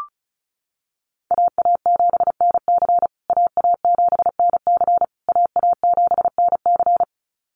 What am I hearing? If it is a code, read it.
Morse code, heard as a keyed tone, about 34 wpm, 700 Hz. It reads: AA7NC AA7NC AA7NC
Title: USA Call Signs Repeated 3x 35wpm